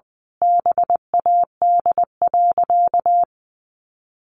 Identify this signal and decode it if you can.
Morse code, heard as a keyed tone, about 20 wpm, 700 Hz. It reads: BAD.
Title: Sentences from Top 1200 Words 20wpm